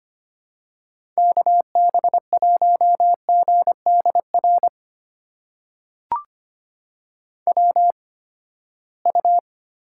Morse code, heard as a keyed tone, about 25 wpm, spaced as for 17 wpm, 700 Hz. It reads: KB1GDR WU
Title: USA Call Signs - 8x Character Spacing - 25wpm